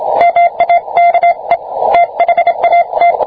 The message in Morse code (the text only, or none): MAKE THAT